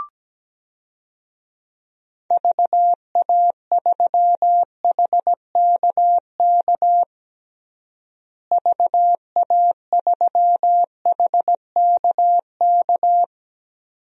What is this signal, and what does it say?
Morse code, heard as a keyed tone, about 17 wpm, 700 Hz. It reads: VA3HKK VA3HKK
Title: DXCC-100 Call Signs - 6 Characters - Repeated 2x - 17wpm